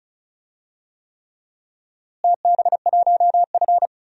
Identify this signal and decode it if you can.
Morse code, heard as a keyed tone, about 34 wpm, 700 Hz. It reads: TB1F
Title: DXCC-100 Call Signs - 4 Characters - Repeated 3x - 35wpm